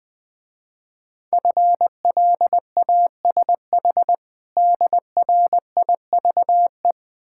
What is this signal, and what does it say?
Morse code, heard as a keyed tone, about 20 wpm, 700 Hz. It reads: FLASH DRIVE
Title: Easy Words - Speed-Racing with Hints - 20wpm